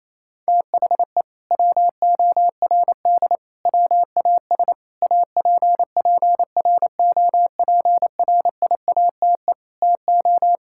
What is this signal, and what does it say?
Morse code, heard as a keyed tone, about 28 wpm, 700 Hz. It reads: THE WORD WAS APPROPRIATE TO
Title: Sentences from Top 1000 Words - Review Part 5 - 28wpm